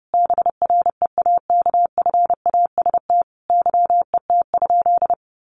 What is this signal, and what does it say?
Morse code, heard as a keyed tone, about 28 wpm, 700 Hz. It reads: BREAKFAST YET?
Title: Sentences from Top 1500 Words 30wpm